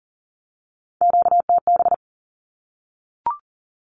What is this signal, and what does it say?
Morse code, heard as a keyed tone, about 40 wpm, 700 Hz. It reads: QTB
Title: Q-Codes Repeated 3x 40wpm